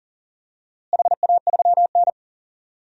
Morse code, heard as a keyed tone, about 40 wpm, 700 Hz. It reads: HA3N
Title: DXCC-100 Call Signs - 4 Characters - Repeated 3x - 40wpm